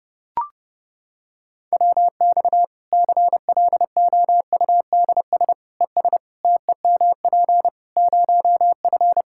Morse code, heard as a keyed tone, about 30 wpm, 700 Hz. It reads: WX CLOUDS ES TEMP 0F
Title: Practice QSO - Exchange 2 - Info - 30wpm